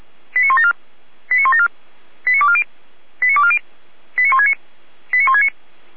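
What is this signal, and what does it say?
Five-tone selective call (ZVEI), heard as quick runs of five tones, tones of about 70 ms. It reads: 89174 89174 89240 89240 89169 89169
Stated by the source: ZVEI five-tone selective call 89168, received on SDR.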